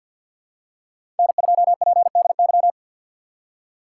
Morse code, heard as a keyed tone, about 49 wpm, 700 Hz. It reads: N2PDY
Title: USA Call Signs - 5 Characters - 2x Character Spacing - 50wpm